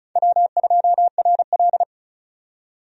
Morse code, heard as a keyed tone, about 34 wpm, 700 Hz. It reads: W2RL
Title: USA Call Signs - 4 Characters - 35wpm